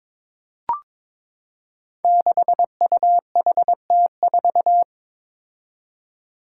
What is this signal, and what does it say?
Morse code, heard as a keyed tone, about 22 wpm, 700 Hz. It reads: 6UHT4
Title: Five Challenging Letter-Number Combinations 22wpm